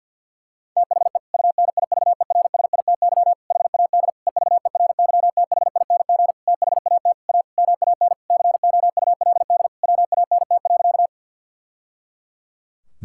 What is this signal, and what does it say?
Morse code, heard as a keyed tone, about 50 wpm, 700 Hz. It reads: THE UNIVERSITY HAD EVERYTHING THAT A MAN COULD WANT.
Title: Sentences from Top 1000 Words - Review Part 5 - 50wpm